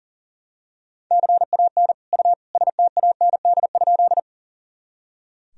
Morse code, heard as a keyed tone, about 40 wpm, 700 Hz. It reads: CAN U STAND?
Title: Sentences from Top 300 Words and Abbreviations 40wpm